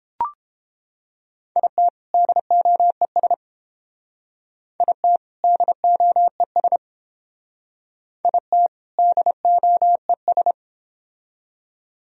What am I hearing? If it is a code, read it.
Morse code, heard as a keyed tone, about 30 wpm, 700 Hz. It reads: IT DOES IT DOES IT DOES